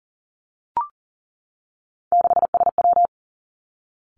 Morse code, heard as a keyed tone, about 38 wpm, 700 Hz. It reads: 6SW
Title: Three Challenging Letter-Number Combinations 40wpm